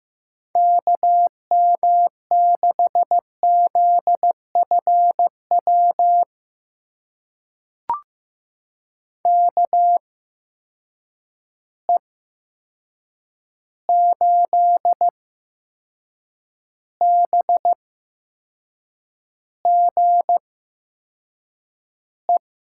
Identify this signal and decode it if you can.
Morse code, heard as a keyed tone, about 15 wpm, spaced as for 6 wpm, 700 Hz. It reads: KM6ZFW KE8BGE
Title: USA Call Signs - 6 Characters - 8x Character Spacing - 15wpm